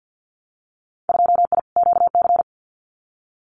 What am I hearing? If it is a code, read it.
Morse code, heard as a keyed tone, about 50 wpm, 700 Hz. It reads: 2I QC
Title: Winter Field Day Exchanges 50wpm